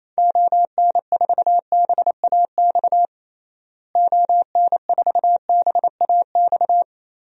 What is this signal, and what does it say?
Morse code, heard as a keyed tone, about 28 wpm, 700 Hz. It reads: ON4BAX ON4BAX
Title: DXCC-100 Call Signs - 6 Characters - Repeated 2x - 28wpm